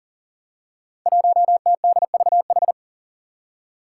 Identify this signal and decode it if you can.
Morse code, heard as a keyed tone, about 40 wpm, 700 Hz. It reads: 1TDVH
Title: Five Challenging Letter-Number Combinations 40wpm